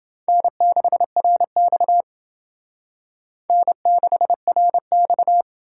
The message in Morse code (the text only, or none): N6RX N6RX